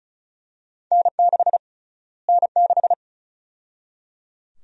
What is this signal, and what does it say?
Morse code, heard as a keyed tone, about 36 wpm, 700 Hz. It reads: N6 N6
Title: USA Call Sign Prefixes - Repeated 2x - 35wpm